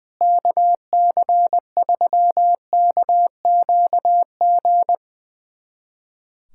Morse code, heard as a keyed tone, about 21 wpm, 700 Hz. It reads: KC3KQG
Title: USA Callsign DE Callsign - 6 Characters - 20wpm